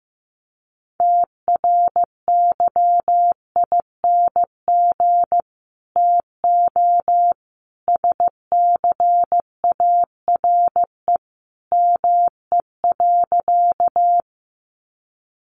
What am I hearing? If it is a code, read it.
Morse code, heard as a keyed tone, about 15 wpm, 700 Hz. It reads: TRYING TO SCARE ME.
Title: A Sound Of Thunder 15wpm